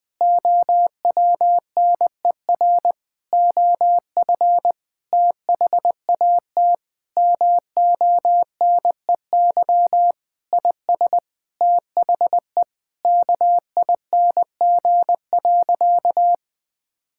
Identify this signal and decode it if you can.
Morse code, heard as a keyed tone, about 20 wpm, 700 Hz. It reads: OWNER OF THAT MONEY IS THE KING.
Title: Sentences from Top 1600 Words 20wpm